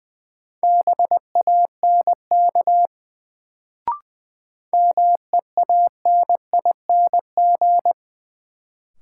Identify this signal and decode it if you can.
Morse code, heard as a keyed tone, about 20 wpm, 700 Hz. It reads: BANK MEANING
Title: Sentences from Top 900 Words - New Word Drill - 20wpm